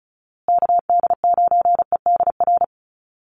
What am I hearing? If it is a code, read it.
Morse code, heard as a keyed tone, about 34 wpm, 700 Hz. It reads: KD9EDR
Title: USA Call Signs Repeated 2x 35wpm